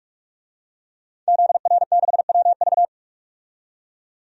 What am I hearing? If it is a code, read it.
Morse code, heard as a keyed tone, about 46 wpm, 700 Hz. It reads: ZR6WV
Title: DXCC-100 Call Signs - 5 Characters - 45wpm